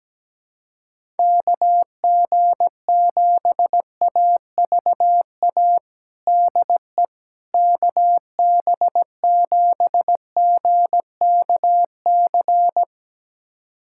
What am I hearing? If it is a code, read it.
Morse code, heard as a keyed tone, about 17 wpm, 700 Hz. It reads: KG7AVA DE KB7GKC
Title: USA Callsign DE Callsign - 6 Characters - 17wpm